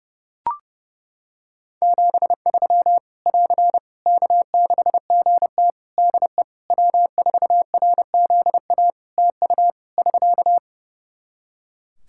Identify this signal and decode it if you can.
Morse code, heard as a keyed tone, about 30 wpm, 700 Hz. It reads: 73 <AR> K6GT DE W4RZA TU <SK>
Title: Practice QSO - Exchange 4 - Ending - 30wpm